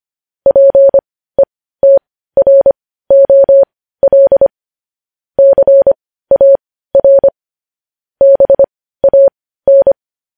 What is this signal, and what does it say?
Morse code, heard as a keyed tone, about 25 wpm, spaced as for 15 wpm, 550 Hz. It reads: PETROL CAR BAN
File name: podcast / News headlines-1x-25x15